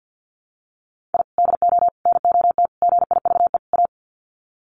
Morse code, heard as a keyed tone, about 50 wpm, 700 Hz. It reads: I DO NOT GIVE A
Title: Sets of 5 Words Encore2 50wpm